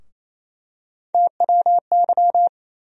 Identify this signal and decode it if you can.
Morse code, heard as a keyed tone, about 28 wpm, 700 Hz. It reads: TWY
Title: USA Call Sign Postfixes - 3 Letters 28wpm